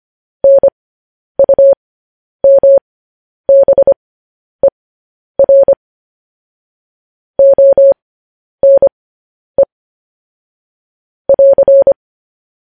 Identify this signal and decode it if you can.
Morse code, heard as a keyed tone, about 25 wpm, spaced as for 10 wpm, 550 Hz. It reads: NUMBER ONE <AR>
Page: news-headlines-in-morse-code-at-10-wpm